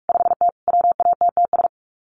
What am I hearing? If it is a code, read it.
Morse code, heard as a keyed tone, about 44 wpm, 700 Hz. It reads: 5T WATTS
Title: Common QSO Phrases 45wpm